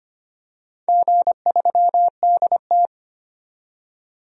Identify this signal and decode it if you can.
Morse code, heard as a keyed tone, about 25 wpm, 700 Hz. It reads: G3DT
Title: DXCC-100 Call Signs - 4 Characters - 25wpm